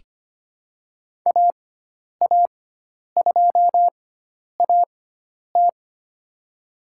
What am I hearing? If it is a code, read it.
Morse code, heard as a keyed tone, about 25 wpm, spaced as for 10 wpm, 700 Hz. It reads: AA2AT
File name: Lesson 16g - Letter c - Call Signs 25wpm